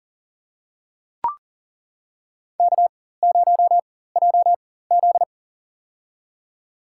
Morse code, heard as a keyed tone, about 40 wpm, spaced as for 19 wpm, 700 Hz. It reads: K0JZ
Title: USA Call Signs - 4 Characters - 4x Character Spacing - 40wpm